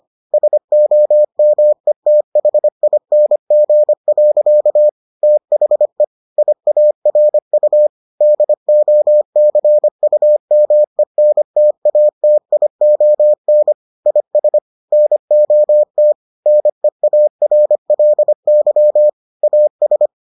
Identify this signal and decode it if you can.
Morse code, heard as a keyed tone, about 25 wpm, 600 Hz. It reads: SOMETHING. THE IARU DOCUMENTATION IS NOT NEARLY AS